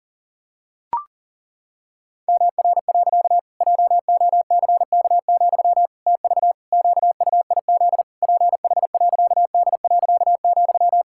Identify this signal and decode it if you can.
Morse code, heard as a keyed tone, about 40 wpm, 700 Hz. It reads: MR. JOCK, TV QUIZ PH.D.,
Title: Pangram Challenge v2 - 40wpm